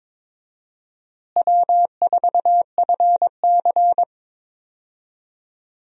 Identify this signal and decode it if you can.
Morse code, heard as a keyed tone, about 22 wpm, 700 Hz. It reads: W4FC
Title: USA Call Signs - 4 Characters - Repeated 2x - 22wpm